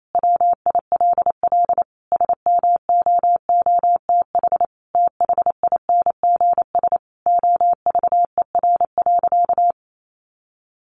Morse code, heard as a keyed tone, about 28 wpm, 700 Hz. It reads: WILL SMOOTH THINGS OVER.
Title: Sentences from Top 1400 Words 28wpm